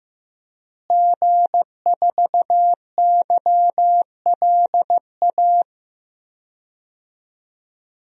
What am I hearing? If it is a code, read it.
Morse code, heard as a keyed tone, about 15 wpm, 700 Hz. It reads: G4YLA
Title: DXCC-100 Call Signs - 5 Characters - Repeated 3x - 15wpm